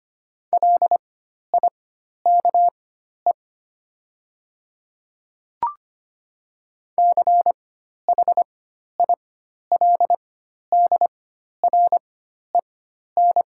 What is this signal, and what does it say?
Morse code, heard as a keyed tone, about 25 wpm, spaced as for 12 wpm, 700 Hz. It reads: LIKE CHILDREN
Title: 4x Character Spacing - Top 300 Words - 25wpm